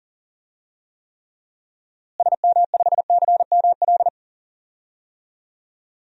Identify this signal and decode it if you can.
Morse code, heard as a keyed tone, about 40 wpm, 700 Hz. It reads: SM5CML
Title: DXCC-100 Call Signs - 6 Characters - Repeated 3x - 40wpm